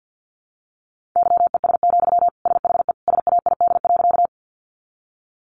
Morse code, heard as a keyed tone, about 50 wpm, 700 Hz. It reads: YES, SHE SAID.
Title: Sentences from Top 500 Words 50wpm